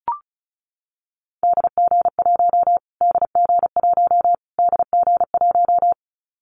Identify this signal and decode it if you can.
Morse code, heard as a keyed tone, about 34 wpm, 700 Hz. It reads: DG1 DG1 DG1
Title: Callsign Prefixes for Top 25 DXCC - Repeated 3x - 35wpm